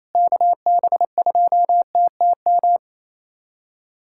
Morse code, heard as a keyed tone, about 28 wpm, 700 Hz. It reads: KB2TTM
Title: USA Callsign DE Callsign - 6 Characters - 28wpm